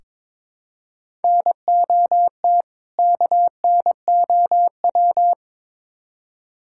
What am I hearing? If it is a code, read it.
Morse code, heard as a keyed tone, about 22 wpm, 700 Hz. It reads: NOT KNOW
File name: practice / 22wpm / Build A Phrase - 5-Word Phrases - 22wpm